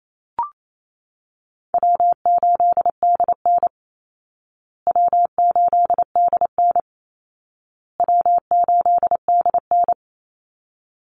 Morse code, heard as a keyed tone, about 28 wpm, 700 Hz. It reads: W8DN W8DN W8DN
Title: Easy Call Signs Repeated 3x - 28wpm